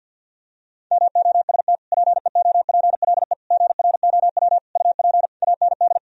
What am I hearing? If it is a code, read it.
Morse code, heard as a keyed tone, about 50 wpm, 700 Hz. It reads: MOST PEOPLE GROW UP AND
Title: Sentences from Top 1000 Words - Review Part 1 - 50wpm